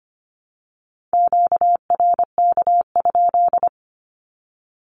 Morse code, heard as a keyed tone, about 25 wpm, 700 Hz. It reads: QRK?
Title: Q-Codes 25wpm